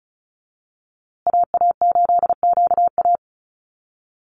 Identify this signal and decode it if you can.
Morse code, heard as a keyed tone, about 34 wpm, 700 Hz. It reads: AA8QA
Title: USA Call Signs Repeated 3x 35wpm